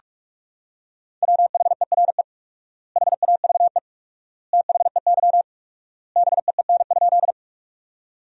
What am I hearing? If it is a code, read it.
Morse code, heard as a keyed tone, about 42 wpm, 700 Hz. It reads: WHERE HAVE THEY BEEN?
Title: 4x Word-Spacing - Sentences from Top 200 Words - 45wpm